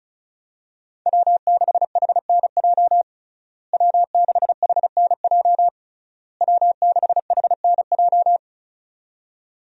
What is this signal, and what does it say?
Morse code, heard as a keyed tone, about 36 wpm, 700 Hz. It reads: W6HNJ W6HNJ W6HNJ
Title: USA Call Signs - 5 Characters - Repeated 3x - 35wpm